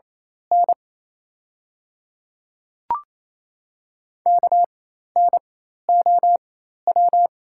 Morse code, heard as a keyed tone, about 27 wpm, spaced as for 13 wpm, 690 Hz. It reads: N KNOW
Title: 4x Character Spacing - Top 100 Words - 28wpm